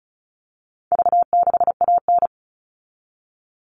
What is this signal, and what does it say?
Morse code, heard as a keyed tone, about 34 wpm, 700 Hz. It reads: V6AN
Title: All Cut Numbers - 4-Digit 35wpm